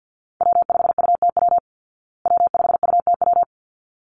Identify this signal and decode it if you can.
Morse code, heard as a keyed tone, about 48 wpm, 700 Hz. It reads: W5UTW W5UTW
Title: Call Signs Repeated 3x - 50wpm